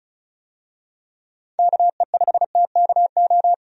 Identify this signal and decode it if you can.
Morse code, heard as a keyed tone, about 36 wpm, 700 Hz. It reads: KE5TKO
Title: USA Call Signs - 6 Characters - Repeated 2x - 35wpm